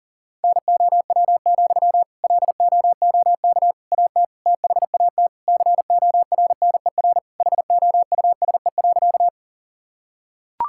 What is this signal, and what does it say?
Morse code, heard as a keyed tone, about 40 wpm, 700 Hz. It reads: NOW, LOOK AT THAT CORNER HOUSE.
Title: Sentences from Top 1600 Words 40wpm